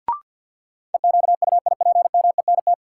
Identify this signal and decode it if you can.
Morse code, heard as a keyed tone, about 47 wpm, 700 Hz. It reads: EQUIPMENT
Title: Sentences from Top 700 Words - New Word Drill - 50wpm